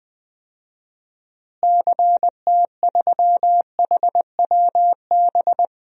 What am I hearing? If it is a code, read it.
Morse code, heard as a keyed tone, about 20 wpm, 700 Hz. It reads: CT3HWB
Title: DXCC-100 Call Signs - 6 Characters - Repeated 3x - 20wpm